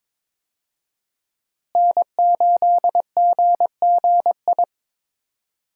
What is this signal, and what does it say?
Morse code, heard as a keyed tone, about 21 wpm, 700 Hz. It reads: N8GGI
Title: USA Call Signs - 5 Characters - Repeated 2x - 22wpm